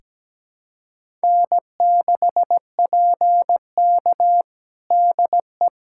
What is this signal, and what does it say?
Morse code, heard as a keyed tone, about 17 wpm, 700 Hz. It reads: N6PK DE
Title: USA Callsign DE Callsign - 4 Characters - 17wpm